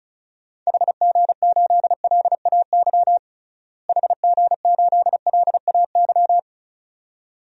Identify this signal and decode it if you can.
Morse code, heard as a keyed tone, about 34 wpm, 700 Hz. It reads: HG8LAY HG8LAY
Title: DXCC-100 Call Signs - 6 Characters - Repeated 2x - 35wpm